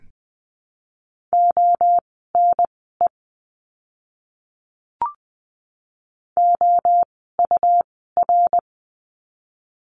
Morse code, heard as a keyed tone, about 20 wpm, spaced as for 14 wpm, 700 Hz. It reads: ONE OUR
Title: 237 Lesson 38f - Letter q - Words 20wpm